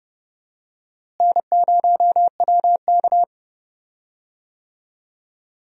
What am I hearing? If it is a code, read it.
Morse code, heard as a keyed tone, about 30 wpm, 700 Hz. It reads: N0WK
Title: USA Call Signs - 4 Characters - Repeated 2x - 30wpm